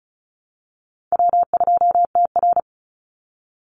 Morse code, heard as a keyed tone, about 34 wpm, 700 Hz. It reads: W2TR
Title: Easy Call Signs Repeated 3x - 35wpm